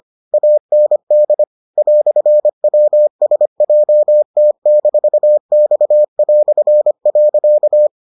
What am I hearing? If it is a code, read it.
Morse code, heard as a keyed tone, about 25 wpm, 600 Hz. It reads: AND "WSJT-X".